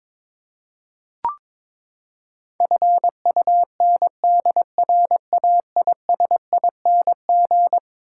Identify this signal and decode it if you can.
Morse code, heard as a keyed tone, about 22 wpm, 700 Hz. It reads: FUNDRAISING
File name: practice / 22wpm / Eleven Letter Words - 22wpm